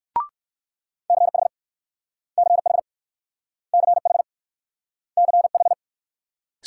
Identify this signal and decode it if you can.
Morse code, heard as a keyed tone, about 56 wpm, 700 Hz. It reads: CH CH CH CH
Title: Speed-Racing Multi-letter Phonemes - 45wpm